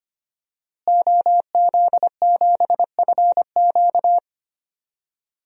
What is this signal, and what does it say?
Morse code, heard as a keyed tone, about 25 wpm, 700 Hz. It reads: OZ7FQ
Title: DXCC-100 Call Signs - 5 Characters - Repeated 2x - 25wpm